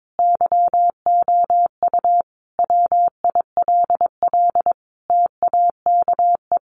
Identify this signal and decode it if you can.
Morse code, heard as a keyed tone, about 22 wpm, 700 Hz. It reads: YOU WILL TAKE
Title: Sentences from Top 900 Words 22wpm